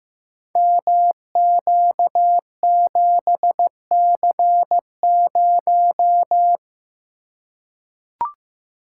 Morse code, heard as a keyed tone, about 15 wpm, 700 Hz. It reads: MQ7C0
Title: Code Groups - 5 Letter:Numbers - 15wpm